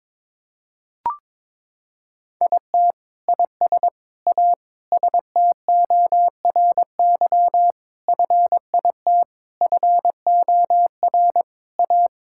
Morse code, heard as a keyed tone, about 22 wpm, 700 Hz. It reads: IT IS A STORY FIT FOR A
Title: Sentences from Top 1000 Words - Review Part 5 - 22wpm